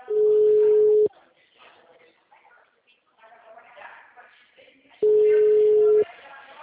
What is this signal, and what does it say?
Ringback tone (Europe, 425 Hz), with speech.